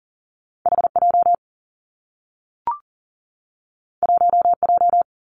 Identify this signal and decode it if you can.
Morse code, heard as a keyed tone, about 42 wpm, 700 Hz. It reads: H J 1 J
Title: Two Challenging Letter-Number Combinations 40wpm